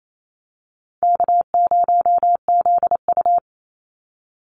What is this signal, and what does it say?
Morse code, heard as a keyed tone, about 28 wpm, 700 Hz. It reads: K0ZU
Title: Easy Call Signs - 28wpm